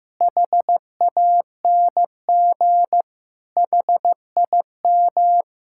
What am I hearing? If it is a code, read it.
Morse code, heard as a keyed tone, about 16 wpm, 700 Hz. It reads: HANG HIM
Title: Sentences from Top 1600 Words 15wpm